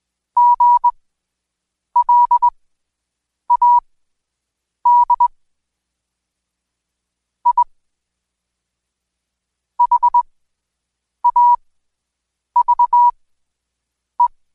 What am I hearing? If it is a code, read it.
Morse code, heard as a keyed tone, about 21 wpm, spaced as for 7 wpm, 970 Hz. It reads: GLAD I HAVE
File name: Code Quick / AudioDownld2 / Seg112-Day17-DayDrill 17